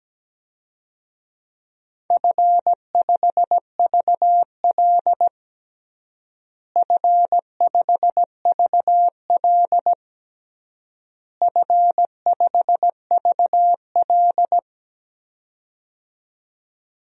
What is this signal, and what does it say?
Morse code, heard as a keyed tone, about 17 wpm, 700 Hz. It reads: F5VL F5VL F5VL
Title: DXCC-100 Call Signs - 4 Characters - Repeated 3x - 17wpm